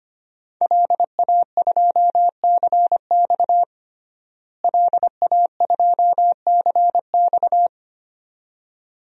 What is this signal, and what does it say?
Morse code, heard as a keyed tone, about 25 wpm, 700 Hz. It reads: LA2CX LA2CX
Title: DXCC-100 Call Signs - 5 Characters - Repeated 2x - 25wpm